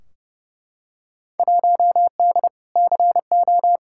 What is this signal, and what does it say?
Morse code, heard as a keyed tone, about 29 wpm, 700 Hz. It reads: 1D CO
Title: Field Day Exchanges - 30wpm